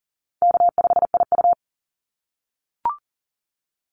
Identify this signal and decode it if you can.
Morse code, heard as a keyed tone, about 42 wpm, 700 Hz. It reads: K5IU
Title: Call Signs - 40wpm